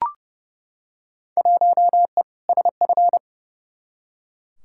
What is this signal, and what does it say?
Morse code, heard as a keyed tone, about 30 wpm, 700 Hz. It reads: 1E SF
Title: Field Day Exchanges - 30wpm